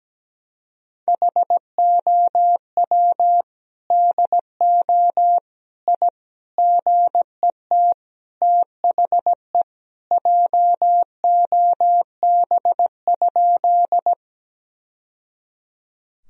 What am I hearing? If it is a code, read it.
Morse code, heard as a keyed tone, about 17 wpm, 700 Hz. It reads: HOW DO I GET THE JOB?
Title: Sentences from Top 1000 Words - Review Part 3 - 17wpm